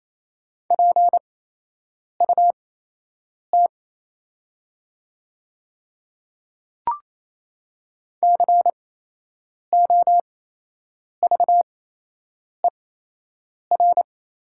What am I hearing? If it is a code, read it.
Morse code, heard as a keyed tone, about 28 wpm, spaced as for 8 wpm, 700 Hz. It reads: PUT COVER
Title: 8x Character Spacing - Top 300 Words - 28wpm